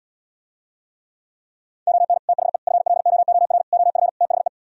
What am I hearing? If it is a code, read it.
Morse code, heard as a keyed tone, about 77 wpm, 700 Hz. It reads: ON THE FLOOR OF THE